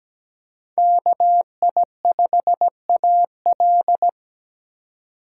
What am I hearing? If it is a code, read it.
Morse code, heard as a keyed tone, about 17 wpm, 700 Hz. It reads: KI5AL
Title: K1USN SST QSO Practice - 17wpm